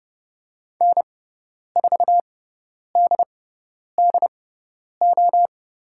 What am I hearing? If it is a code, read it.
Morse code, heard as a keyed tone, about 30 wpm, spaced as for 10 wpm, 700 Hz. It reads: N4DDO
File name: Lesson 16g - Letter c - Call Signs 30wpm